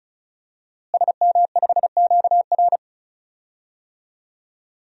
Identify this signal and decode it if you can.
Morse code, heard as a keyed tone, about 36 wpm, 700 Hz. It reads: SM5QR
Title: DXCC-100 Call Signs - 5 Characters - Repeated 3x - 35wpm